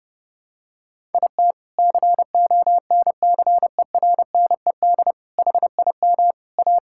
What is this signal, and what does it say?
Morse code, heard as a keyed tone, about 31 wpm, 700 Hz. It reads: IT CONCERNED HIM A
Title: Sentences from Top 1000 Words - Review Part 6 - 30wpm